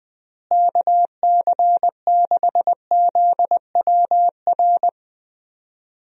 Morse code, heard as a keyed tone, about 20 wpm, 700 Hz. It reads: KC6ZWR
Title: USA Call Signs - 6 Characters - Repeated 3x - 20wpm